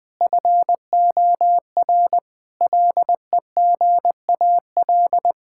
Morse code, heard as a keyed tone, about 20 wpm, 700 Hz. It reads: FOR LEGAL
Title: Sets of 8 Words - 20wpm